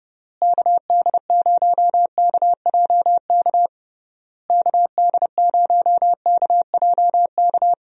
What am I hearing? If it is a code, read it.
Morse code, heard as a keyed tone, about 32 wpm, 700 Hz. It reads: KD0KJK KD0KJK
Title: USA Call Signs - 6 Characters - Repeated 3x - 30wpm